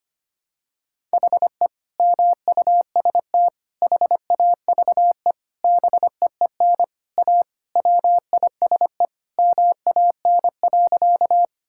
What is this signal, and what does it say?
Morse code, heard as a keyed tone, about 25 wpm, 700 Hz. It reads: HE MUST HAVE BEEN A WISE MAN.
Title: Sentences from Top 1600 Words 25wpm